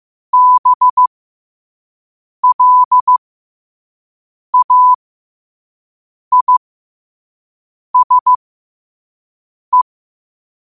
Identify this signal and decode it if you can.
Morse code, heard as a keyed tone, about 15 wpm, spaced as for 5 wpm, 1000 Hz. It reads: BLAISE
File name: qotdfiles / 5wpm